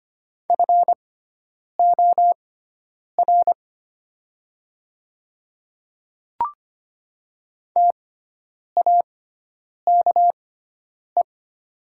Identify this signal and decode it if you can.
Morse code, heard as a keyed tone, about 25 wpm, spaced as for 9 wpm, 700 Hz. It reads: FOR TAKE
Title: 6x Character Spacing - Top 300 Words - 25wpm